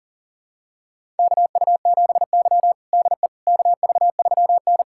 Morse code, heard as a keyed tone, about 40 wpm, 700 Hz. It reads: KU7Y DE KV3N
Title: USA Callsign DE Callsign - 4 Characters - 40wpm